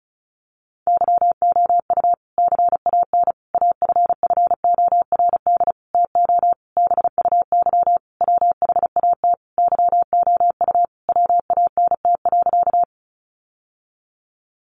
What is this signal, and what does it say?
Morse code, heard as a keyed tone, about 34 wpm, 700 Hz. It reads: YOU CAN AFFORD TO BUY WHAT YOU WANT.
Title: Sentences from Top 1100 Words 35wpm